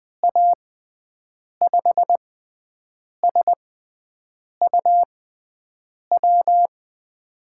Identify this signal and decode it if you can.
Morse code, heard as a keyed tone, about 20 wpm, spaced as for 7 wpm, 700 Hz. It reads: A5SUW
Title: USA Call Signs - 6x Character Spacing - 20wpm